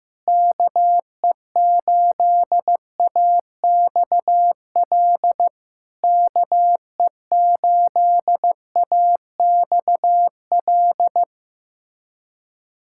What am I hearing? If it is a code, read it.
Morse code, heard as a keyed tone, about 16 wpm, 690 Hz. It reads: KE8AXL KE8AXL
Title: SOTA QSO Practice 15wpm